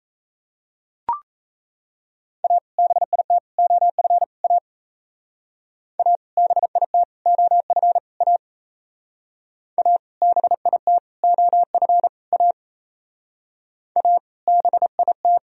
Morse code, heard as a keyed tone, about 36 wpm, 700 Hz. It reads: A BIT OF A A BIT OF A A BIT OF A A BIT